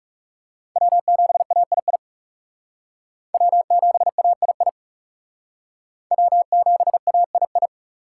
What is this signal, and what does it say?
Morse code, heard as a keyed tone, about 40 wpm, 690 Hz. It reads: W7AII W7AII W7AII